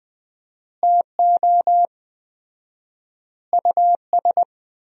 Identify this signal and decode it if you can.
Morse code, heard as a keyed tone, about 20 wpm, 700 Hz. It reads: TO US
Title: 4x Word-Spacing - Sentences from Top 500 Words - 20wpm